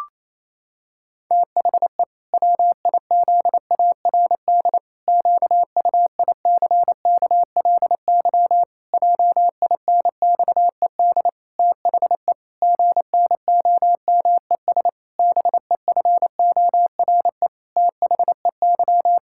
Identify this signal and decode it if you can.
Morse code, heard as a keyed tone, about 28 wpm, 700 Hz. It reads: THE WIZARD QUICKLY JINXED THE GNOMES BEFORE THEY
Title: Pangram Challenge v2 - 28wpm